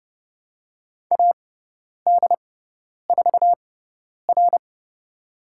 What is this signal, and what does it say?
Morse code, heard as a keyed tone, about 30 wpm, spaced as for 10 wpm, 700 Hz. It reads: AD4R
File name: Lesson 17g - Letter u - Call Signs 30wpm